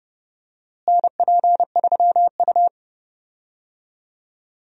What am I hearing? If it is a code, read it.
Morse code, heard as a keyed tone, about 32 wpm, 700 Hz. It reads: NP3U
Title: USA Call Signs - 4 Characters - Repeated 2x - 30wpm